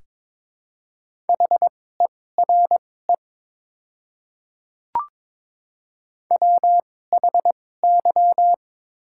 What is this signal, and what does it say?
Morse code, heard as a keyed tone, about 22 wpm, spaced as for 16 wpm, 700 Hz. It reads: HERE WHY
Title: 2x Character Spacing - Top 300 Words - 22wpm